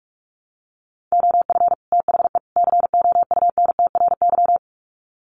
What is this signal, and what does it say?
Morse code, heard as a keyed tone, about 44 wpm, 700 Hz. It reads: OF THE COUNTRY
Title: Sets of 3 Words 45wpm